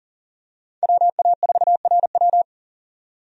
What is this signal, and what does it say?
Morse code, heard as a keyed tone, about 40 wpm, 700 Hz. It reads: WA4RW
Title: USA Call Signs - 5 Characters - 40wpm